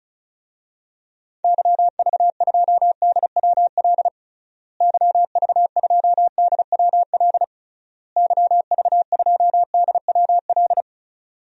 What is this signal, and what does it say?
Morse code, heard as a keyed tone, about 36 wpm, 700 Hz. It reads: YV2DWL YV2DWL YV2DWL
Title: DXCC-100 Call Signs - 6 Characters - Repeated 3x - 35wpm